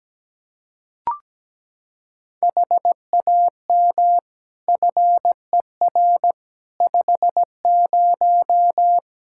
Morse code, heard as a keyed tone, about 17 wpm, 700 Hz. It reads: HAM FER 50
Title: Practice QSO - Exchange 3 - More Info - 17wpm